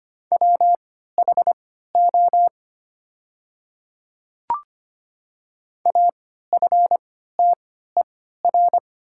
Morse code, heard as a keyed tone, about 25 wpm, spaced as for 14 wpm, 690 Hz. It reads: WHO AFTER
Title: 3x Character Spacing - Top 200 Words - 25wpm